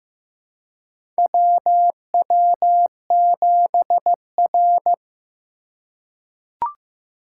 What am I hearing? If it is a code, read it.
Morse code, heard as a keyed tone, about 15 wpm, 700 Hz. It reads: WW7R
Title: USA Call Signs - 4 Characters - 4x Character Spacing - 15wpm